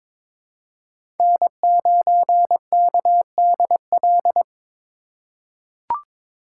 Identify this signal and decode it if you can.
Morse code, heard as a keyed tone, about 22 wpm, 700 Hz. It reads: N9KDL
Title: USA Call Signs - 2x Character Spacing - 22wpm